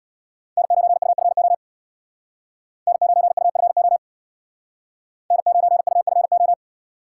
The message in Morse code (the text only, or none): N9UFO N9UFO N9UFO